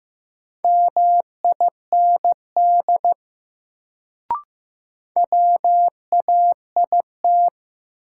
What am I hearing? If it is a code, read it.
Morse code, heard as a keyed tone, about 15 wpm, 700 Hz. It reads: MIND WAIT
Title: Sentences from Top 500 Words - New Word Drill - 15wpm